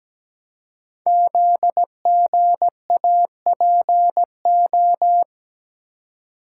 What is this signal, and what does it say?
Morse code, heard as a keyed tone, about 17 wpm, 700 Hz. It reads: ZGAPO
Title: Code Groups - 5 Letters - 17wpm